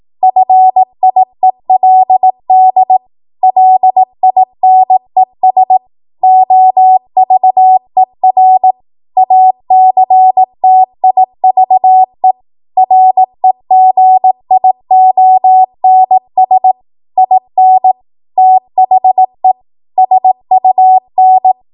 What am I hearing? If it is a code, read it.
Morse code, heard as a keyed tone, about 18 wpm, 750 Hz. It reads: FIELD LINES OVER ACTIVE REGIONS IN THE SUN